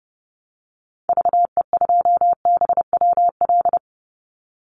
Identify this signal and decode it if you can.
Morse code, heard as a keyed tone, about 30 wpm, 700 Hz. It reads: VE2BWL
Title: Call Signs Repeated 3x - 30wpm